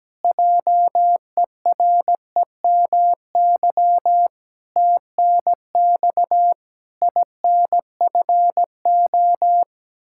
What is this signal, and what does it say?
Morse code, heard as a keyed tone, about 17 wpm, 700 Hz. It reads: JEREMY TNX INFO
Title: Practice QSO - Exchange 3 - More Info - 17wpm